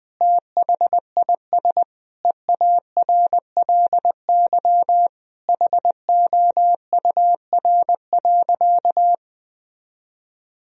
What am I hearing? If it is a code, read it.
Morse code, heard as a keyed tone, about 20 wpm, 700 Hz. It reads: THIS EARLY HOUR.
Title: Sentences from Top 1000 Words - Review Part 2 - 20wpm